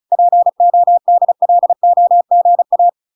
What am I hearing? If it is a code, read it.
Morse code, heard as a keyed tone, about 35 wpm, 700 Hz. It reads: PODLOGA